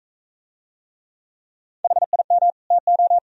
Morse code, heard as a keyed tone, about 39 wpm, 700 Hz. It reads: HIM TO